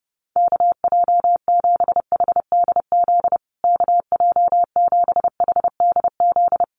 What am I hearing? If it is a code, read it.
Morse code, heard as a keyed tone, about 30 wpm, 700 Hz. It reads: KJ7HDZ KJ7HDZ
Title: USA Call Signs Repeated 3x 30wpm